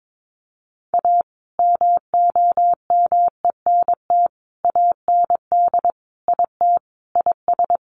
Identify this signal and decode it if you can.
Morse code, heard as a keyed tone, about 22 wpm, 700 Hz. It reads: A MOMENT AND IT IS
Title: Sentences from Top 1000 Words 22wpm